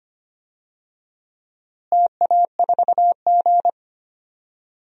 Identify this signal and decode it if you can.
Morse code, heard as a keyed tone, about 25 wpm, 700 Hz. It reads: TA4G
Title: DXCC-100 Call Signs - 4 Characters - Repeated 3x - 25wpm